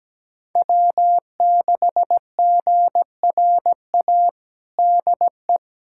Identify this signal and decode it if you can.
Morse code, heard as a keyed tone, about 17 wpm, 700 Hz. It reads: W6GRA DE
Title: USA Callsign DE Callsign 17wpm